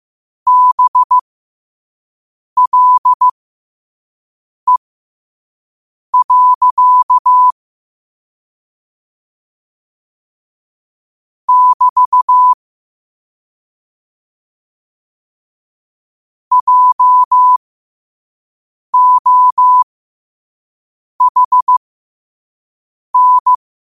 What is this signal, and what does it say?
Morse code, heard as a keyed tone, about 15 wpm, spaced as for 5 wpm, 1000 Hz. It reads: BLE. <BT> JOHN
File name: 5wpm